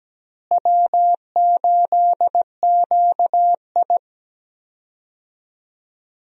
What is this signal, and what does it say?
Morse code, heard as a keyed tone, about 17 wpm, 700 Hz. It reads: W8QI
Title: USA Call Signs - 4 Characters - Repeated 3x - 17wpm